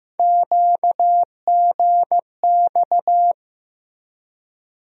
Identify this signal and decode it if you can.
Morse code, heard as a keyed tone, about 15 wpm, 700 Hz. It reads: QGX
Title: USA Call Sign Postfixes - 3 Letters 15wpm